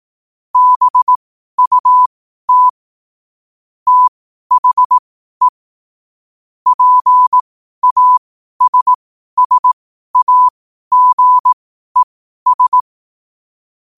Morse code, heard as a keyed tone, about 18 wpm, spaced as for 12 wpm, 1000 Hz. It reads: BUT THE PASSAGES
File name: qotdfiles / 12wpm